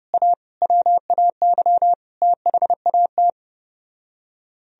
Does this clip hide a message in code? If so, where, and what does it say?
Yes: Morse code from 0.1 to 3.3 s, heard as a keyed tone, about 32 wpm, 700 Hz. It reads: A WAY THAT